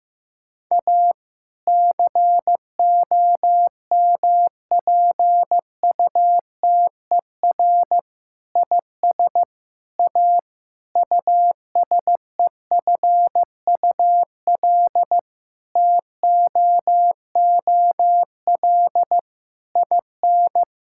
Morse code, heard as a keyed tone, about 16 wpm, 700 Hz. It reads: A COMPUTER IS A USEFUL TOOL IN
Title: Sentences from Top 1000 Words - Review Part 6 - 15wpm